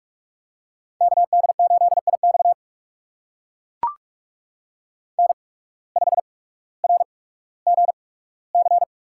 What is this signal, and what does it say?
Morse code, heard as a keyed tone, about 42 wpm, spaced as for 19 wpm, 700 Hz. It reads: KD8IX N5RGC
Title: USA Call Signs - 5 Characters - 8x Character Spacing - 45wpm